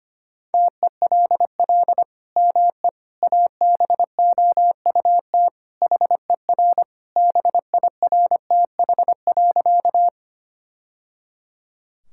Morse code, heard as a keyed tone, about 25 wpm, 700 Hz. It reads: TELL ME ABOUT HER BIRTH.
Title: Sentences from Top 1000 Words - Review Part 5 - 25wpm